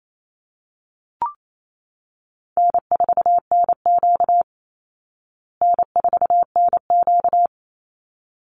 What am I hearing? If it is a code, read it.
Morse code, heard as a keyed tone, about 28 wpm, 700 Hz. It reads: N4NQ N4NQ
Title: Call Signs Repeated 3x - 28wpm